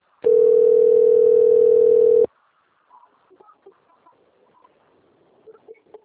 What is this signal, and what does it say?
Ringback tone (North America), 440 + 480 Hz.